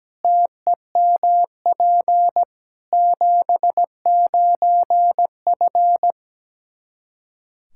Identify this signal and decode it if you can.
Morse code, heard as a keyed tone, about 17 wpm, 700 Hz. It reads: TEMP 79F
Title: Practice QSO - Exchange 2 - Info - 17wpm